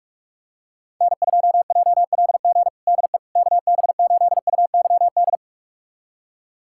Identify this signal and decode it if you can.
Morse code, heard as a keyed tone, about 46 wpm, 690 Hz. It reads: N2JLG DE KB8UYD
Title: USA Callsign DE Callsign 45wpm